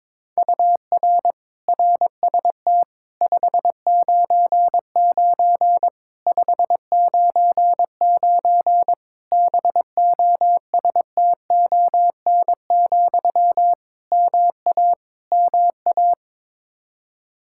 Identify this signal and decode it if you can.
Morse code, heard as a keyed tone, about 22 wpm, 700 Hz. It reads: UR RST 599 599 BOSTON, MA MA
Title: Practice QSO - Exchange 1 - Beginning - 22wpm